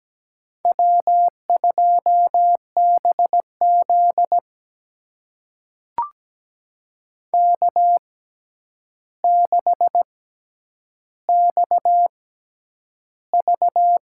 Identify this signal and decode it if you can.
Morse code, heard as a keyed tone, about 17 wpm, spaced as for 9 wpm, 700 Hz. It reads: W2BZ K6XV
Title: USA Call Signs - 4 Characters - 6x Character Spacing - 17wpm